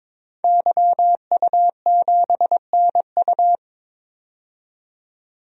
Morse code, heard as a keyed tone, about 22 wpm, 700 Hz. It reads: YU7NU
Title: DXCC-100 Call Signs - 5 Characters - 22wpm